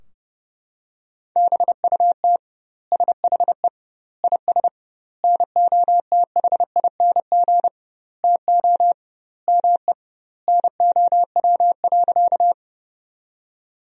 Morse code, heard as a keyed tone, about 32 wpm, 700 Hz. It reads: BUT SHE IS NOTHING TO ME NOW.
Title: 2x Word-Spacing - Sentences from Top 500 Words - 30wpm